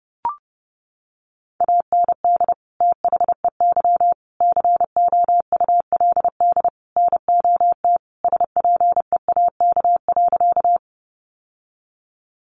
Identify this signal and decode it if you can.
Morse code, heard as a keyed tone, about 29 wpm, 700 Hz. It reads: AND THEY COULD NOT SPEAK.
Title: Sentences from Top 1100 Words 30wpm